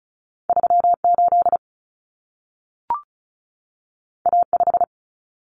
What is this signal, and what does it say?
Morse code, heard as a keyed tone, about 36 wpm, 700 Hz. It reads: 38 A5
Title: Common Cut Numbers - 2-Digit 35wpm